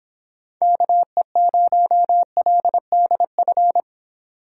Morse code, heard as a keyed tone, about 26 wpm, 700 Hz. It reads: KE0LDF